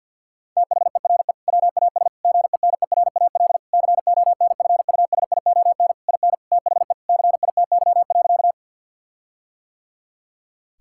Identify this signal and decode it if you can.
Morse code, heard as a keyed tone, about 50 wpm, 700 Hz. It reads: THERE WAS GENERAL CONFUSION IN THE CITY.
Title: Sentences from Top 1600 Words 50wpm